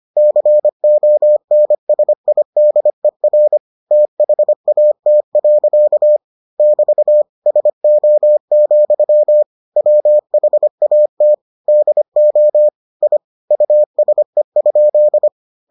Morse code, heard as a keyed tone, about 25 wpm, 600 Hz. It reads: CONSIDER THAT. <BT> SO, WHAT DO I USE?